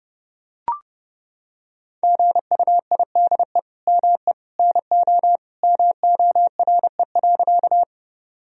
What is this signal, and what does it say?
Morse code, heard as a keyed tone, about 32 wpm, 700 Hz. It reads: GUIDE ME NO MORE.
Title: Sentences from Top 1000 Words - Review Part 3 - 30wpm